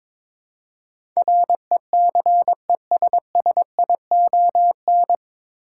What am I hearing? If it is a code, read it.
Morse code, heard as a keyed tone, about 22 wpm, 700 Hz. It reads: RECESSION
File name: 9 Letter Words - First 3 Letters Given - 22wpm